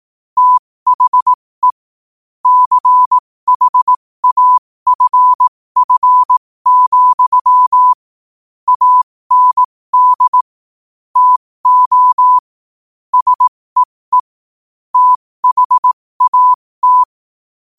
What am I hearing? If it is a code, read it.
Morse code, heard as a keyed tone, about 18 wpm, spaced as for 15 wpm, 1000 Hz. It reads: THE CHAFF, AND TO SEE THAT
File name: qotdfiles / 15wpm